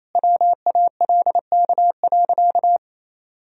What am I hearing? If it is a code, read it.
Morse code, heard as a keyed tone, about 28 wpm, 700 Hz. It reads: WALK.
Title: Sentences from Top 1000 Words - Review Part 1 - 28wpm